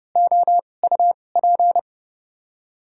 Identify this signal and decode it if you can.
Morse code, heard as a keyed tone, about 30 wpm, spaced as for 22 wpm, 700 Hz. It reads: OUP
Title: 2x Character Spacing - Top 300 Words - 30wpm